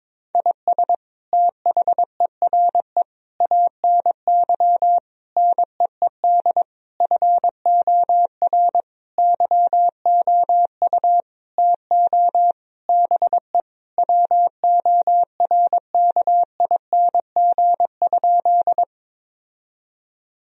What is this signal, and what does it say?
Morse code, heard as a keyed tone, about 22 wpm, 700 Hz. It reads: IS THERE ANY NEED FOR YOU TO BE WORKING?
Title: Sentences from Top 1000 Words - Review Part 4 - 22wpm